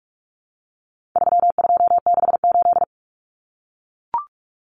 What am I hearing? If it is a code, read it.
Morse code, heard as a keyed tone, about 44 wpm, 700 Hz. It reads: 3268
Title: Numbers - 4-Digit 45wpm